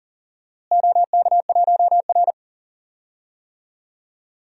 Morse code, heard as a keyed tone, about 40 wpm, 700 Hz. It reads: OK1R
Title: DXCC-100 Call Signs - 4 Characters - Repeated 2x - 40wpm